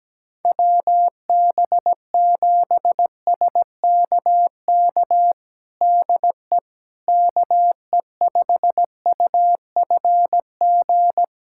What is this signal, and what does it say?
Morse code, heard as a keyed tone, about 17 wpm, 700 Hz. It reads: WB7SKK DE KE5UFG
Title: USA Callsign DE Callsign - 6 Characters - 17wpm